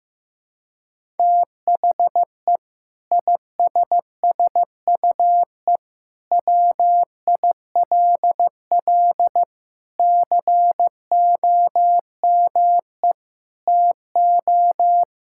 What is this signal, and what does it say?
Morse code, heard as a keyed tone, about 15 wpm, 700 Hz. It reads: THE ISSUE WILL COME TO
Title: Sentences from Top 1000 Words - Review Part 6 - 15wpm